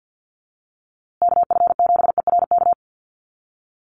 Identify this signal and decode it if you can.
Morse code, heard as a keyed tone, about 48 wpm, 700 Hz. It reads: KF7ERK
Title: USA Call Signs Repeated 3x 50wpm